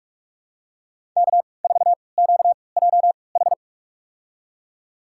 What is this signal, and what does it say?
Morse code, heard as a keyed tone, about 47 wpm, spaced as for 26 wpm, 700 Hz. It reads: K4QJH
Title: USA Call Signs - 5 Characters - 3x Character Spacing - 45wpm